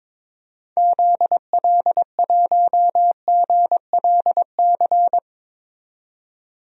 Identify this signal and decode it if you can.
Morse code, heard as a keyed tone, about 22 wpm, 700 Hz. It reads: ZL1GLC
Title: DXCC-100 Call Signs - 6 Characters - 22wpm